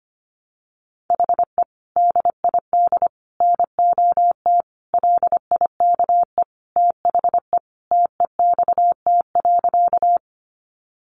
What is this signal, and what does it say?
Morse code, heard as a keyed tone, about 25 wpm, 700 Hz. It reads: HE DID NOT LIKE THE TEXT.
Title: Sentences from Top 1200 Words 25wpm